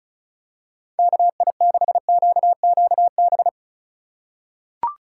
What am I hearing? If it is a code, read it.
Morse code, heard as a keyed tone, about 36 wpm, 690 Hz. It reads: KI6QQB
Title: USA Call Signs - 6 Characters - 4x Character Spacing - 35wpm